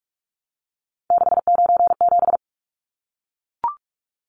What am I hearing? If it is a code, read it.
Morse code, heard as a keyed tone, about 44 wpm, 700 Hz. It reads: 697
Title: Numbers - 3-Digit 45wpm